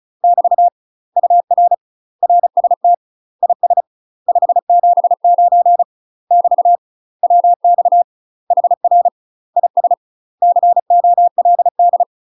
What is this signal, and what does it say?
Morse code, heard as a keyed tone, about 35 wpm, 700 Hz. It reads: <BT> UR RST IS 579 <BT> WX HR IS COLD
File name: UK MASTER QSOs TRAINING / 35WPM / UK MASTER QSOs 0012